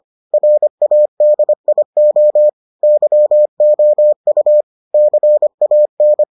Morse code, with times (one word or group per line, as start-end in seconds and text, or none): 0.3-2.5 RADIO
2.8-4.6 YOU
4.9-6.2 CAN